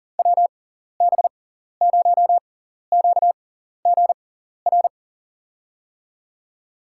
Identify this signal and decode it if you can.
Morse code, heard as a keyed tone, about 40 wpm, spaced as for 14 wpm, 700 Hz. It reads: WB0QGR
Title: USA Call Signs - 6 Characters - 6x Character Spacing - 40wpm